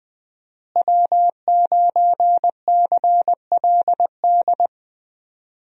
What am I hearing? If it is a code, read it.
Morse code, heard as a keyed tone, about 20 wpm, 700 Hz. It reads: W9CLD
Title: USA Call Signs - 5 Characters - Repeated 2x - 20wpm